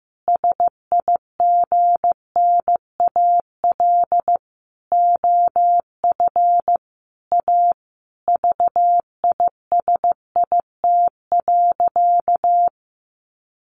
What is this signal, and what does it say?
Morse code, heard as a keyed tone, about 15 wpm, 700 Hz. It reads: SIGNAL OF A VISIT.
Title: Sentences from Top 1400 Words 15wpm